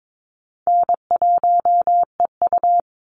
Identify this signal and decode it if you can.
Morse code, heard as a keyed tone, about 22 wpm, 700 Hz. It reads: N1EU
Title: NA SOTA Activator Callsigns 22wpm